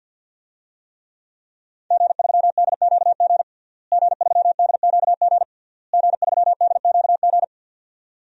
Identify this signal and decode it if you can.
Morse code, heard as a keyed tone, about 52 wpm, 700 Hz. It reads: G3DQG G3DQG G3DQG
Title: DXCC-100 Call Signs - 5 Characters - Repeated 3x - 50wpm